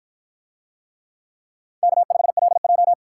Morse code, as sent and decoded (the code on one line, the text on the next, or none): -.- ..... .-.. .---
K5LJ